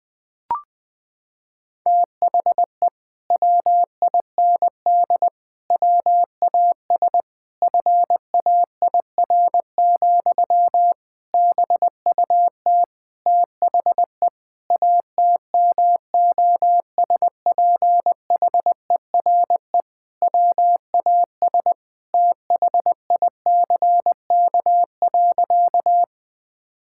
Morse code, heard as a keyed tone, about 20 wpm, 700 Hz. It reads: THE WIND WAS FAIR, BUT THE ATMOSPHERE WAS THICK.
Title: Sentences from Top 1600 Words 20wpm